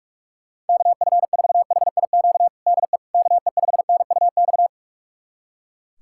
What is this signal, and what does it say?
Morse code, heard as a keyed tone, about 42 wpm, 700 Hz. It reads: KF4HIQ DE KE5NUX
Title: USA Callsign DE Callsign 45wpm